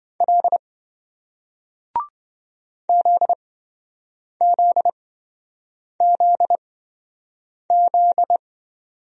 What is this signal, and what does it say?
Morse code, heard as a keyed tone, about 27 wpm, spaced as for 7 wpm, 700 Hz. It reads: L ZZZZ